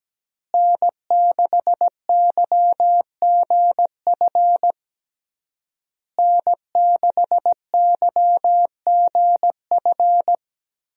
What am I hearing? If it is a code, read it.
Morse code, heard as a keyed tone, about 17 wpm, 700 Hz. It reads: N6YGF N6YGF
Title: USA Call Signs - 5 Characters - Repeated 3x - 17wpm